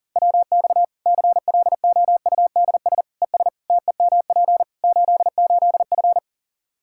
Morse code, heard as a keyed tone, about 40 wpm, 700 Hz. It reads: WX CLOUDS ES TEMP 88F
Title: K1USN SST QSO Practice 40wpm